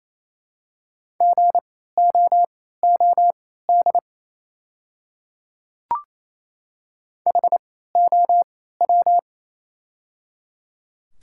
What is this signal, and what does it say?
Morse code, heard as a keyed tone, about 27 wpm, spaced as for 16 wpm, 690 Hz. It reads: GOOD HOW
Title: 3x Character Spacing - Top 500 Words - 28wpm